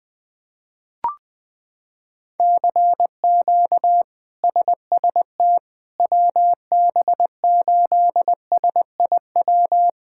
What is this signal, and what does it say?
Morse code, heard as a keyed tone, about 20 wpm, 700 Hz. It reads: CQ SST WB8SIW
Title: K1USN SST QSO Practice - 20wpm